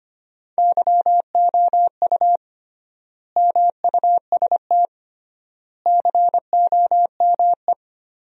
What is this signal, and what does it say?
Morse code, heard as a keyed tone, about 25 wpm, 700 Hz. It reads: YOU MUST COME
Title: 3x Word-Spacing - Sentences from Top 200 Words - 25wpm